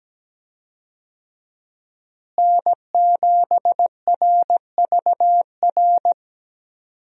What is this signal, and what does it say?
Morse code, heard as a keyed tone, about 17 wpm, 700 Hz. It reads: N7RVR
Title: USA Call Signs - 5 Characters - Repeated 2x - 17wpm